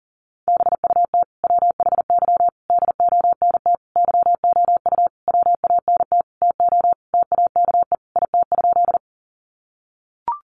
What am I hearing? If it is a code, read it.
Morse code, heard as a keyed tone, about 40 wpm, 700 Hz. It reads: BUT WHY DONT YOU WANT TO TAKE IT?
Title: Sentences from Top 300 Words 40wpm